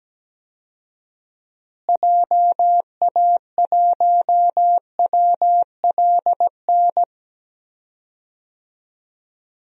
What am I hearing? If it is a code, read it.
Morse code, heard as a keyed tone, about 17 wpm, 700 Hz. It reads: JA1WLN
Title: DXCC-100 Call Signs - 6 Characters - Repeated 2x - 17wpm